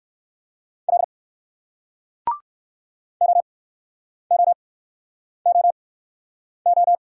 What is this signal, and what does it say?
Morse code, heard as a keyed tone, about 60 wpm, spaced as for 9 wpm, 700 Hz. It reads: L OOOO